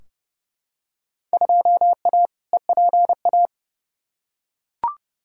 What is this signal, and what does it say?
Morse code, heard as a keyed tone, about 29 wpm, 700 Hz. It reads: 2A EPA
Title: Field Day Exchanges - 30wpm